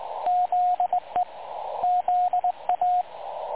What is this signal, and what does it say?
Morse code, heard as a keyed tone, about 19 wpm, 720 Hz, with radio noise. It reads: ZE ZA
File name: dlazaawansowanych / nagraniazpasma / websdr recording 3565.0kHz